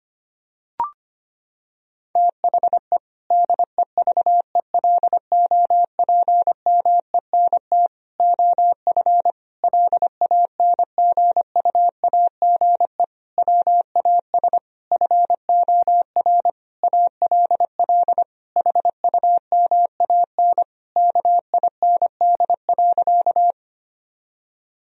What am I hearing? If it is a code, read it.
Morse code, heard as a keyed tone, about 25 wpm, 700 Hz. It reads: THE DEVELOPMENT OF LANGUAGE WAS FOR ALL HUMAN KIND.
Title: Sentences from Top 1000 Words - Review Part 4 - 25wpm